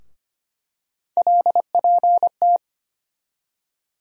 Morse code, heard as a keyed tone, about 25 wpm, 700 Hz. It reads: LPT
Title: USA Call Sign Postfixes - 3 Letters 25wpm